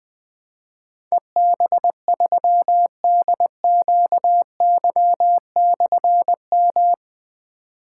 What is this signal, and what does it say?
Morse code, heard as a keyed tone, about 21 wpm, 700 Hz. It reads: EB3DQY/M
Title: Difficult Call Signs 20wpm